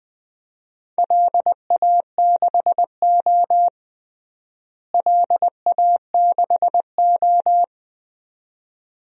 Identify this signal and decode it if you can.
Morse code, heard as a keyed tone, about 20 wpm, 700 Hz. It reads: LA6O LA6O
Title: DXCC-100 Call Signs - 4 Characters - Repeated 2x - 20wpm